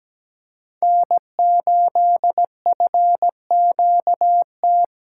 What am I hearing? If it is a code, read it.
Morse code, heard as a keyed tone, about 17 wpm, 700 Hz. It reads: N8FQT
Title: USA Callsign DE Callsign - 5 Characters - 17wpm